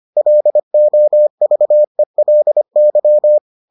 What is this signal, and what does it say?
Morse code, heard as a keyed tone, about 25 wpm, 600 Hz. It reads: LOVELY